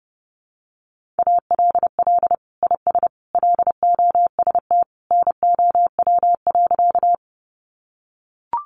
Morse code, heard as a keyed tone, about 30 wpm, 700 Hz. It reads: ALL IS LOST NOW.
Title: Sentences from Top 1400 Words 30wpm